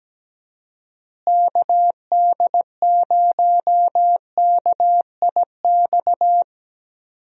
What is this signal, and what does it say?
Morse code, heard as a keyed tone, about 17 wpm, 700 Hz. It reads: KD0KIX
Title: USA Call Signs - 6 Characters - Repeated 3x - 17wpm